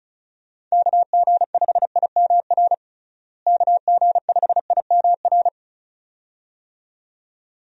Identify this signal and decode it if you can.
Morse code, heard as a keyed tone, about 34 wpm, 700 Hz. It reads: KG5IMR KG5IMR
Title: USA Call Signs - 6 Characters - Repeated 2x - 35wpm